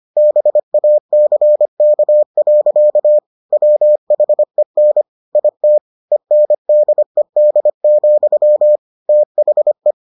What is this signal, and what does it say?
Morse code, heard as a keyed tone, about 25 wpm, 600 Hz. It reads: BACK. WHEN IT ENDED, THE